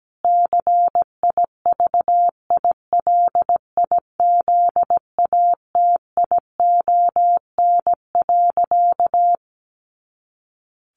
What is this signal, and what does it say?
Morse code, heard as a keyed tone, about 17 wpm, 700 Hz. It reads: CIVILIZATION.
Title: A Sound Of Thunder 17wpm